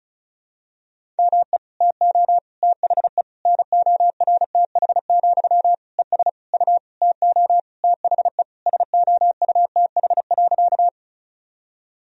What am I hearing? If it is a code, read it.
Morse code, heard as a keyed tone, about 36 wpm, 700 Hz. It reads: ME TO THE NORTH, ES U TO THE SOUTH.
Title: Sentences from Top 400 Words and Abbreviations 35wpm